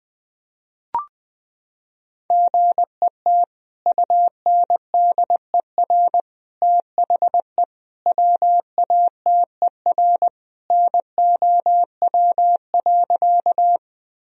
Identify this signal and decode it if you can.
Morse code, heard as a keyed tone, about 21 wpm, 700 Hz. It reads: GET UNDER THE WATER NOW.
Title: Sentences from Top 1000 Words - Review Part 1 - 20wpm